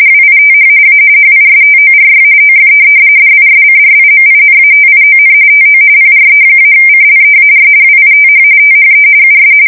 RTTY, about 45.45 baud. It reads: WITH A HISTORY OF DISCRIMINATION TO PRE-CLEAR WITH THE US
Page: 170 Shift 2125/2295